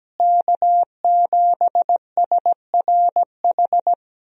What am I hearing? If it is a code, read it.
Morse code, heard as a keyed tone, about 17 wpm, 700 Hz. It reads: K7SRH
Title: USA Call Signs - 5 Characters - Repeated 3x - 17wpm